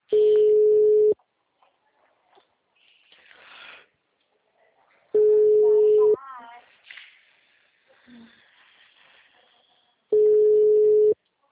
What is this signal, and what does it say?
Ringback tone (Europe, 425 Hz).